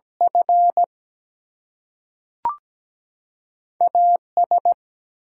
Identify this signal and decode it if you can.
Morse code, heard as a keyed tone, about 17 wpm, 700 Hz. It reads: F AS
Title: Two Letter Words - 17wpm